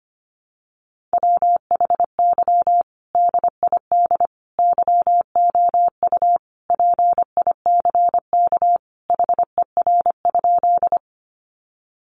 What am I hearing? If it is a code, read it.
Morse code, heard as a keyed tone, about 25 wpm, 700 Hz. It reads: WHY DID YOU PICK HER?
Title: Sentences from Top 1000 Words 25wpm